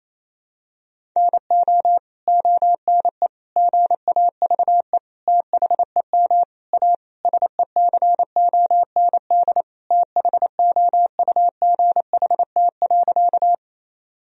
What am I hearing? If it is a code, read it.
Morse code, heard as a keyed tone, about 28 wpm, 700 Hz. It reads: NO ONE GAVE THEM A SECOND THOUGHT.
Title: Sentences from Top 1000 Words - Review Part 1 - 28wpm